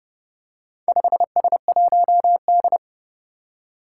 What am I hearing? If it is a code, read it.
Morse code, heard as a keyed tone, about 30 wpm, 700 Hz. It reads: 5S1D
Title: Four Challenging Letter-Number Combinations 30wpm